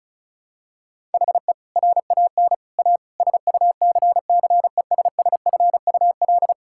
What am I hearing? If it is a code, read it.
Morse code, heard as a keyed tone, about 36 wpm, 700 Hz. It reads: HE RAN A SUCCESSFUL
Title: Sentences from Top 1000 Words - Review Part 6 - 35wpm